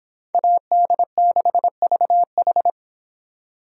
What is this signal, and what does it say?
Morse code, heard as a keyed tone, about 26 wpm, 700 Hz. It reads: AD6VH